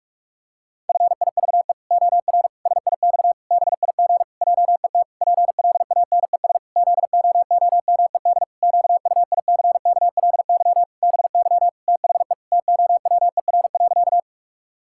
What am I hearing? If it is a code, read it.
Morse code, heard as a keyed tone, about 46 wpm, 700 Hz. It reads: FIVE OR SIX BIG JET PLANES ZOOMED QUICKLY BY THE TOWER.
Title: Pangram Challenge v2 - 45wpm